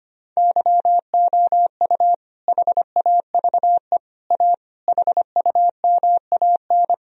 Morse code, heard as a keyed tone, about 25 wpm, 700 Hz. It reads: YOU HAVE A HUMAN
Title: Sentences from Top 1000 Words - Review Part 3 - 25wpm